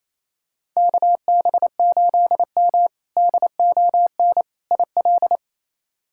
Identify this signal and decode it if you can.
Morse code, heard as a keyed tone, about 27 wpm, 700 Hz. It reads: KB8M DON IL
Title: K1USN SST QSO Practice - 28wpm